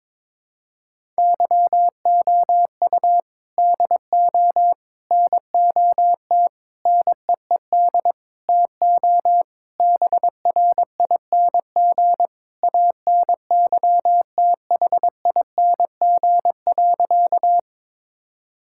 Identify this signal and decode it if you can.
Morse code, heard as a keyed tone, about 22 wpm, 700 Hz. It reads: YOU DO NOT NEED TO BRING ANYTHING.
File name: practice / 22wpm / Sentences with 4-Word Phrases - 22wpm